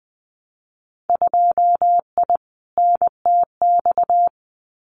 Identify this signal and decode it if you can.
Morse code, heard as a keyed tone, about 20 wpm, 700 Hz. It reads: 2I NTX
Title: Winter Field Day Exchanges Encore 20wpm